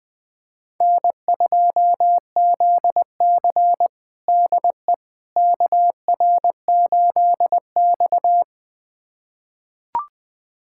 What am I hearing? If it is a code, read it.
Morse code, heard as a keyed tone, about 20 wpm, 700 Hz. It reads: N2ZC DE KR8X
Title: USA Callsign DE Callsign - 4 Characters - 20wpm